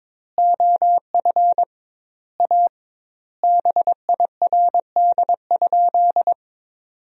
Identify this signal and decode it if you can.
Morse code, heard as a keyed tone, about 22 wpm, 700 Hz. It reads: OF A BIRD?
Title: 2x Word-Spacing - Sentences from Top 400 Words - 22wpm